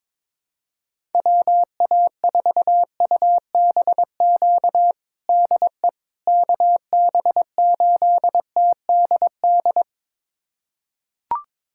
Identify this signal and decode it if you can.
Morse code, heard as a keyed tone, about 22 wpm, 700 Hz. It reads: WA4UBQ DE KB8TDD
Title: USA Callsign DE Callsign - 6 Characters - 22wpm